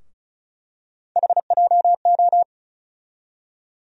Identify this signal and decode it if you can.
Morse code, heard as a keyed tone, about 36 wpm, 700 Hz. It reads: HJO
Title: USA Call Sign Postfixes - 3 Letters 35wpm